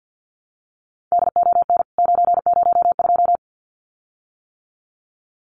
Morse code, heard as a keyed tone, about 52 wpm, 700 Hz. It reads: DON 902
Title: CWT Exchange 50wpm